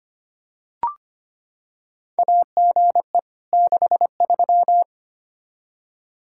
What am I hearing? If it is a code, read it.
Morse code, heard as a keyed tone, about 25 wpm, 700 Hz. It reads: AGE 63
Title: Practice QSO - Exchange 3 - More Info - 25wpm